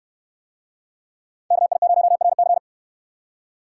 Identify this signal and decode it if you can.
Morse code, heard as a keyed tone, about 69 wpm, 700 Hz. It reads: KI0RC